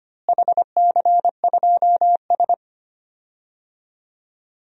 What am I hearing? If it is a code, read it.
Morse code, heard as a keyed tone, about 25 wpm, 700 Hz. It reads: HC2S
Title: DXCC-100 Call Signs - 4 Characters - Repeated 3x - 25wpm